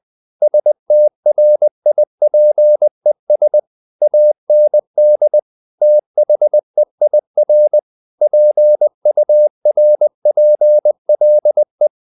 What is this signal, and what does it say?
Morse code, heard as a keyed tone, about 20 wpm, 600 Hz. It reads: STRIPES AND THEIR PURPLE
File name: the old man and the sea part 2 20wpm